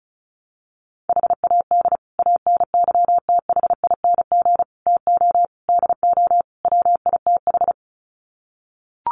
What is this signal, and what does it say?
Morse code, heard as a keyed tone, about 34 wpm, 700 Hz. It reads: HAD ANYTHING TO DO WITH
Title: Sets of 5 Words Encore 35wpm